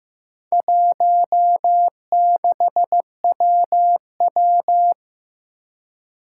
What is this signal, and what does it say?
Morse code, heard as a keyed tone, about 15 wpm, 700 Hz. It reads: 16WW
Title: Four Challenging Letter-Number Combinations 15wpm